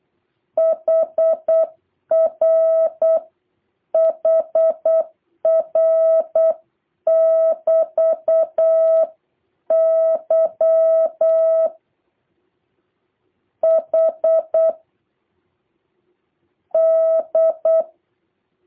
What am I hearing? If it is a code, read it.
Morse code, heard as a keyed tone, about 8 wpm, 650 Hz. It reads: HR HR <BT> Y H D
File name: jh8cbh / 2023cw